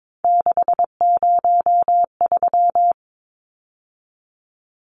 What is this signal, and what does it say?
Morse code, heard as a keyed tone, about 22 wpm, 700 Hz. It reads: 603
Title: Numbers - 3-Digit 22wpm